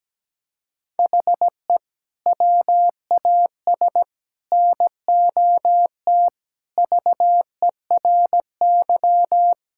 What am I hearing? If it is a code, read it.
Morse code, heard as a keyed tone, about 17 wpm, 700 Hz. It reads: HE WAS NOT VERY
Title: Sentences from Top 1600 Words 17wpm